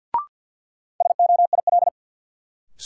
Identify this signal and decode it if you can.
Morse code, heard as a keyed tone, about 46 wpm, 700 Hz. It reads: SOIL
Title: Sentences from Top 1500 Words - New Word Drill Alt - 50wpm